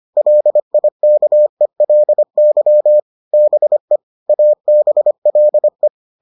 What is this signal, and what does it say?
Morse code, heard as a keyed tone, about 25 wpm, 600 Hz. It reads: LIKELY BE ABLE